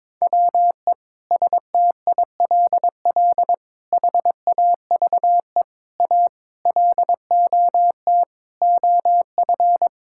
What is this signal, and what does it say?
Morse code, heard as a keyed tone, about 22 wpm, 700 Hz. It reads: WE STILL HAVE A LOT OF
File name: Sets of 9 Words - 22wpm